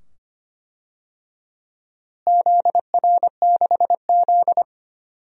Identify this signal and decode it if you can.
Morse code, heard as a keyed tone, about 25 wpm, 700 Hz. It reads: ZR6Z
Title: DXCC-100 Call Signs - 4 Characters - Repeated 2x - 25wpm